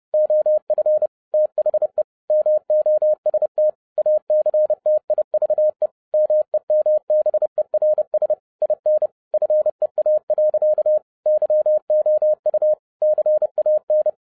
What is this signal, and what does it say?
Morse code, heard as a keyed tone, about 30 wpm, 610 Hz. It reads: OF THE MOST ACTIVE MEMBERS IN FEA. YOU CAN
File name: 30wpm